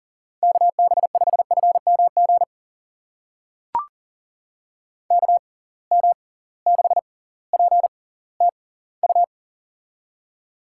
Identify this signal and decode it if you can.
Morse code, heard as a keyed tone, about 40 wpm, spaced as for 20 wpm, 700 Hz. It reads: KB5FMG KM6PTU
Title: USA Call Signs - 6 Characters - 6x Character Spacing - 40wpm